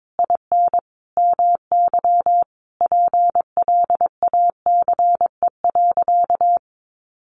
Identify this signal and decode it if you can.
Morse code, heard as a keyed tone, about 22 wpm, 700 Hz. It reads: IN MY PLACE.
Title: Sentences from Top 1200 Words 22wpm